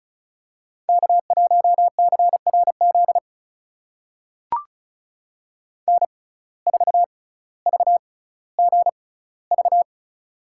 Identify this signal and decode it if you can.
Morse code, heard as a keyed tone, about 35 wpm, spaced as for 18 wpm, 700 Hz. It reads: K1CRZ N4VGV
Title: USA Call Signs - 5 Characters - 6x Character Spacing - 35wpm